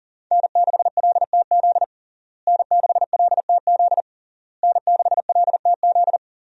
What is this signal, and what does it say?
Morse code, heard as a keyed tone, about 40 wpm, 690 Hz. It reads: N6LTZ N6LTZ N6LTZ
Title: USA Call Signs - 5 Characters - Repeated 3x - 40wpm